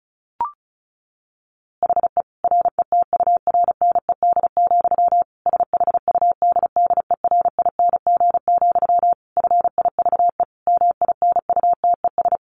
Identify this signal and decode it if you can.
Morse code, heard as a keyed tone, about 34 wpm, 700 Hz. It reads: HE RETURNED, SHUDDERING, FIVE MINUTES
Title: A Sound Of Thunder 35wpm